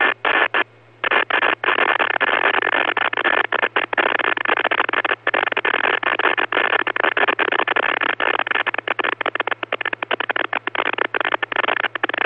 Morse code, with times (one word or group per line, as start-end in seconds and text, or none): none